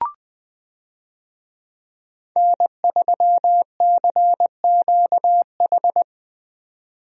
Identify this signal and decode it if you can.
Morse code, heard as a keyed tone, about 20 wpm, 700 Hz. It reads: N3CQH
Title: USA Call Signs - 5 Characters - 20wpm